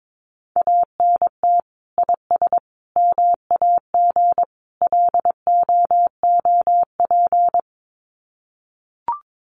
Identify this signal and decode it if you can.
Morse code, heard as a keyed tone, about 22 wpm, 700 Hz. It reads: ANT IS MAG LOOP
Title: Common QSO Phrases 22wpm